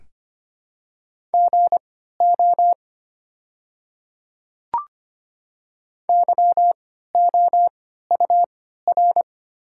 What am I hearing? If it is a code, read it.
Morse code, heard as a keyed tone, about 25 wpm, spaced as for 14 wpm, 700 Hz. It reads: GO YOUR
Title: 3x Character Spacing - Top 100 Words - 25wpm